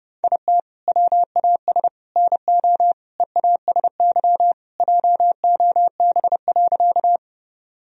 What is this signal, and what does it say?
Morse code, heard as a keyed tone, about 32 wpm, 700 Hz. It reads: IT WAS NO EASY JOB.
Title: Sentences from Top 1000 Words - Review Part 3 - 30wpm